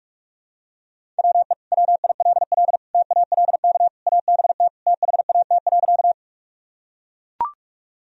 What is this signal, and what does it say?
Morse code, heard as a keyed tone, about 41 wpm, 700 Hz. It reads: WE WILL TALK ABT THAT.
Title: Sentences from Top 400 Words and Abbreviations 45wpm